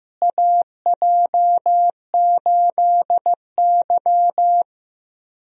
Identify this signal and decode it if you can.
Morse code, heard as a keyed tone, about 15 wpm, 700 Hz. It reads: AJ8Y
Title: USA Call Signs - 4 Characters - Repeated 3x - 15wpm